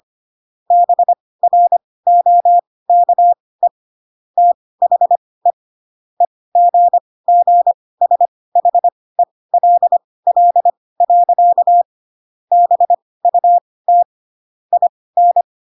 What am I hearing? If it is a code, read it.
Morse code, heard as a keyed tone, about 25 wpm, spaced as for 18 wpm, 700 Hz. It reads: BROKE THE EGGSHELL. BUT IN